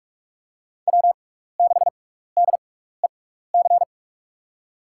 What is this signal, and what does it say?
Morse code, heard as a keyed tone, about 46 wpm, spaced as for 16 wpm, 700 Hz. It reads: W6DEC
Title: USA Call Signs - 5 Characters - 6x Character Spacing - 45wpm